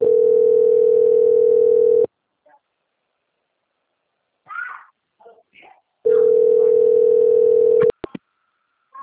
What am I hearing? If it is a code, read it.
Ringback tone (North America), 440 + 480 Hz.